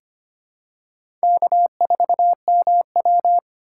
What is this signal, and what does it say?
Morse code, heard as a keyed tone, about 25 wpm, 700 Hz. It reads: K4MW
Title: USA Call Signs - 4 Characters - 2x Character Spacing - 25wpm